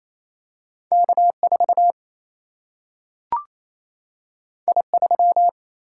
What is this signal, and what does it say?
Morse code, heard as a keyed tone, about 28 wpm, 700 Hz. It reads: K 4 I 3
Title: Single-Letter Single-Number 28wpm